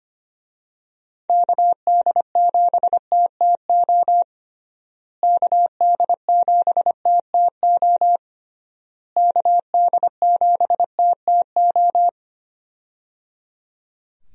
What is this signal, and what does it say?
Morse code, heard as a keyed tone, about 25 wpm, 700 Hz. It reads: KD7TTO KD7TTO KD7TTO
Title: USA Call Signs - 6 Characters - Repeated 3x - 25wpm